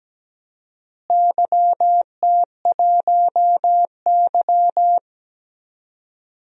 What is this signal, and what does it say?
Morse code, heard as a keyed tone, about 17 wpm, 700 Hz. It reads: YT1Y
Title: DXCC-100 Call Signs - 4 Characters - Repeated 2x - 17wpm